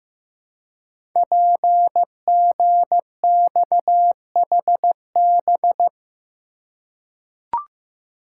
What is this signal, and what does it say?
Morse code, heard as a keyed tone, about 15 wpm, 700 Hz. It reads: PGXHB
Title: Code Groups - 5 Letters - 15wpm